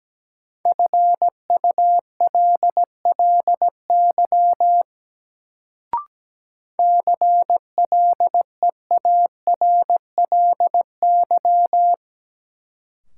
Morse code, heard as a keyed tone, about 17 wpm, 700 Hz. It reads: FULLY CLEARLY
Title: Sentences from Top 1300 Words - New Word Drill - 17wpm